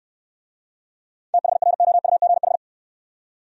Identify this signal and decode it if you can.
Morse code, heard as a keyed tone, about 68 wpm, 700 Hz. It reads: THROUGH